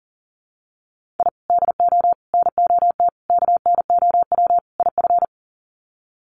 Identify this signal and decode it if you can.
Morse code, heard as a keyed tone, about 37 wpm, 700 Hz. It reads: I DO NOT KNOW IF
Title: Sets of 5 Words 40wpm